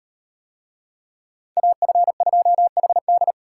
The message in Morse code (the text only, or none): AF2HD